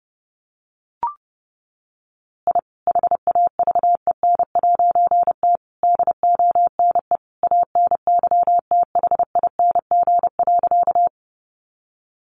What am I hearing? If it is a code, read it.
Morse code, heard as a keyed tone, about 29 wpm, 700 Hz. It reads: I HAVEN'T DONE ANYTHING.
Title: A Sound Of Thunder 30wpm